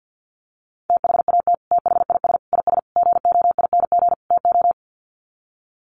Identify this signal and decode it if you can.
Morse code, heard as a keyed tone, about 50 wpm, 700 Hz. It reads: THAT THIS IS GOING TO
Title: Sets of 5 Words Encore2 50wpm